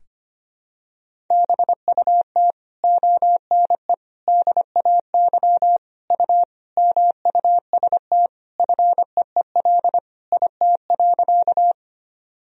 Morse code, heard as a keyed tone, about 25 wpm, 700 Hz. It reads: BUT ONE DAY U MUST FEEL IT.
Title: Sentences from Top 400 Words and Abbreviations 25wpm